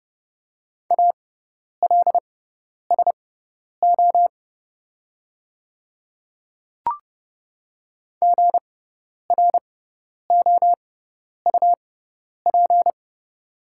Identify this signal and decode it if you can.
Morse code, heard as a keyed tone, about 30 wpm, spaced as for 10 wpm, 700 Hz. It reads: ALSO GROUP
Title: 6x Character Spacing - Top 500 Words - 30wpm